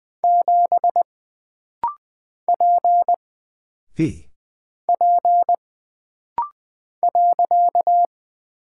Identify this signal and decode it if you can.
Morse code, heard as a keyed tone, about 20 wpm, 700 Hz. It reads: 7 P P .
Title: Single Character (Letter:Number:Punctuation:Prosign) - Rapid-Fire - 20wpm